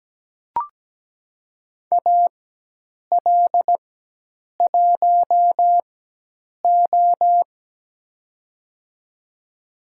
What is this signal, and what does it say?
Morse code, heard as a keyed tone, about 17 wpm, spaced as for 8 wpm, 700 Hz. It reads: AL1O
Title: USA Call Signs - 4 Characters - 4x Character Spacing - 17wpm